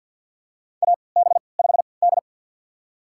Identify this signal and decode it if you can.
Morse code, heard as a keyed tone, about 50 wpm, spaced as for 28 wpm, 700 Hz. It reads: AB5D
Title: USA Call Signs - 4 Characters - 3x Character Spacing - 50wpm